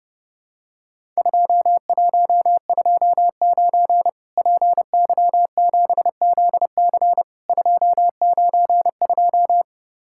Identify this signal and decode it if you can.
Morse code, heard as a keyed tone, about 31 wpm, 700 Hz. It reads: 2129 PY7ZC 292
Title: Worked All Europe DX Contest - QTC Exchange - 30wpm